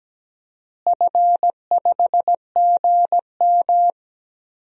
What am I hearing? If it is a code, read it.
Morse code, heard as a keyed tone, about 17 wpm, 700 Hz. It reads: F5GM
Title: DXCC-100 Call Signs - 4 Characters - Repeated 3x - 17wpm